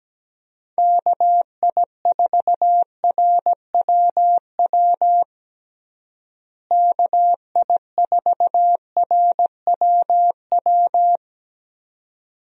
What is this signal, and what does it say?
Morse code, heard as a keyed tone, about 17 wpm, 700 Hz. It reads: KI4RWW KI4RWW
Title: USA Call Signs - 6 Characters - Repeated 2x - 17wpm